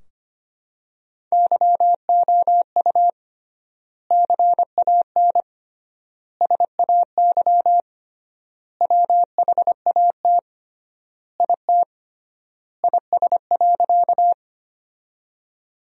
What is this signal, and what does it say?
Morse code, heard as a keyed tone, about 25 wpm, 700 Hz. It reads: YOU CAN SAY WHAT IT IS.
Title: 3x Word-Spacing - Sentences from Top 200 Words - 25wpm